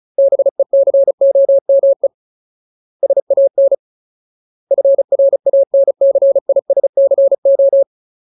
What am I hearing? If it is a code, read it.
Morse code, heard as a keyed tone, about 34 wpm, 550 Hz. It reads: BECOME SAN FRANCISCO
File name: Ultimate Word List / 35WPM Ultimate